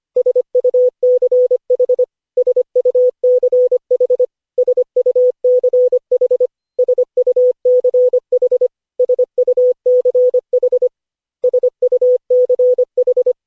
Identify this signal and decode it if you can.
Morse code, heard as a keyed tone, about 24 wpm, 500 Hz. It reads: SUCH SUCH SUCH SUCH SUCH SUCH
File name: cw / Session5 cw training